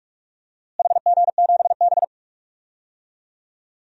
Morse code, heard as a keyed tone, about 46 wpm, 700 Hz. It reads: HG7B
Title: DXCC-100 Call Signs - 4 Characters - Repeated 2x - 45wpm